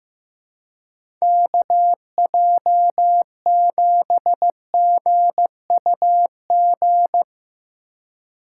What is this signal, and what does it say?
Morse code, heard as a keyed tone, about 15 wpm, 700 Hz. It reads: KJ7GUG
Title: USA Call Signs - 2x Character Spacing - 15wpm